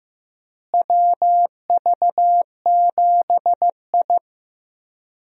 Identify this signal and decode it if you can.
Morse code, heard as a keyed tone, about 15 wpm, 700 Hz. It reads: WV7I
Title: USA Call Signs - 4 Characters - Repeated 3x - 15wpm